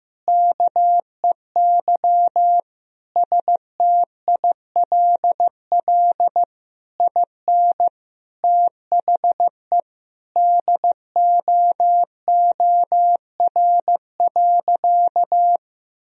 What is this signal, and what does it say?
Morse code, heard as a keyed tone, about 16 wpm, 700 Hz. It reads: KEY STILL IN THE DOOR.
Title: Sentences from Top 1000 Words - Review Part 4 - 15wpm